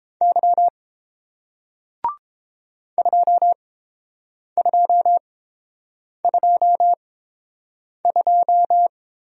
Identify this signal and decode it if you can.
Morse code, heard as a keyed tone, about 31 wpm, spaced as for 7 wpm, 700 Hz. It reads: Y 2222